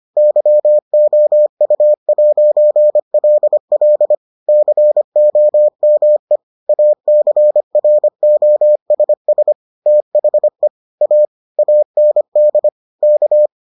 Morse code, heard as a keyed tone, about 25 wpm, 600 Hz. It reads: YOU'LL COME ACROSS THE A AND K